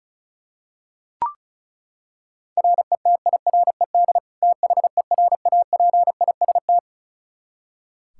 Morse code, heard as a keyed tone, about 34 wpm, 700 Hz. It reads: RETIRED THERAPIST
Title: Practice QSO - Exchange 3 - More Info - 35wpm